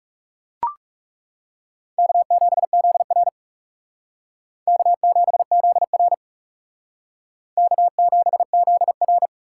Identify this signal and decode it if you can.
Morse code, heard as a keyed tone, about 36 wpm, 700 Hz. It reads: K7ZR K7ZR K7ZR
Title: Speed Racing - USA 1x2 Call Signs 30wpm